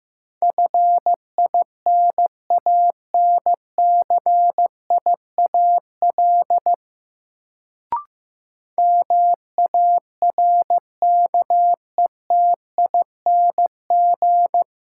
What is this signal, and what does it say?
Morse code, heard as a keyed tone, about 15 wpm, 700 Hz. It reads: FINANCIAL MARKETING
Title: Sentences from Top 1000 Words - Review Part 5 - New Word Drill - 15wpm